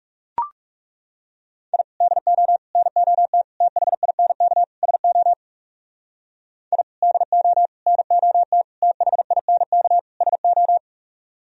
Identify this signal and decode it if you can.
Morse code, heard as a keyed tone, about 38 wpm, 700 Hz. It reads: I DO NOT THINK SO I DO NOT THINK SO